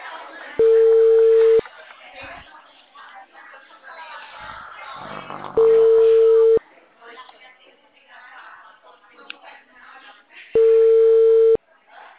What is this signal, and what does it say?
Ringback tone (China, 450 Hz), with speech.